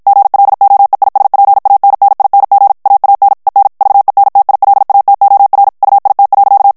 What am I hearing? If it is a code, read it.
Morse code, heard as a keyed tone, about 56 wpm, 800 Hz. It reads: GLOEISPANNING AAN EN VENTILATOR UIT.
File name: morse mp3 / HOLLANDAIS texte clair 2 vitesse 50 mots minute